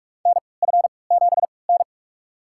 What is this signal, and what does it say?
Morse code, heard as a keyed tone, about 48 wpm, spaced as for 26 wpm, 700 Hz. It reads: NF7N
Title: USA Call Signs - 4 Characters - 3x Character Spacing - 45wpm